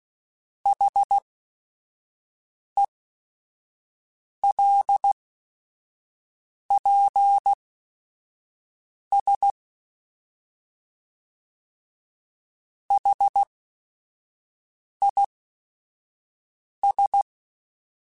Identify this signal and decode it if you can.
Morse code, heard as a keyed tone, about 16 wpm, spaced as for 5 wpm, 800 Hz. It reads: HELPS HIS